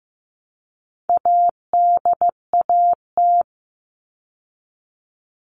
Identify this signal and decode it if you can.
Morse code, heard as a keyed tone, about 15 wpm, 700 Hz. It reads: ADAT
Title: All Cut Numbers - 4-Digit 15wpm